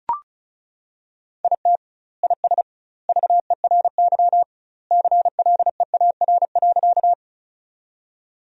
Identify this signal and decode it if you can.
Morse code, heard as a keyed tone, about 34 wpm, 700 Hz. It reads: IT IS VERY CLEAR.
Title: 2x Word-Spacing - Sentences from Top 500 Words - 35wpm